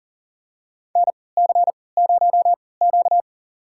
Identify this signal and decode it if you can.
Morse code, heard as a keyed tone, about 40 wpm, spaced as for 23 wpm, 700 Hz. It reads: NC0Q
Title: USA Call Signs - 4 Characters - 3x Character Spacing - 40wpm